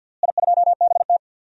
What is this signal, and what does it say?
Morse code, heard as a keyed tone, about 52 wpm, 700 Hz. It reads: I2BT
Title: DXCC-100 Call Signs - 4 Characters - Repeated 2x - 50wpm